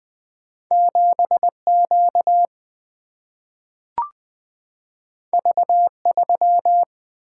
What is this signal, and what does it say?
Morse code, heard as a keyed tone, about 20 wpm, 700 Hz. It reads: 7Q V3
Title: Two Letter-Number Combinations - 20wpm